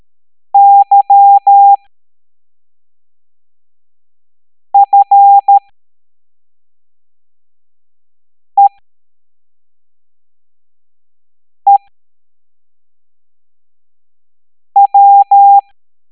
Morse code, heard as a keyed tone, about 13 wpm, spaced as for 3 wpm, 800 Hz. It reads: YFEEW
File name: hamdata / L09-EX04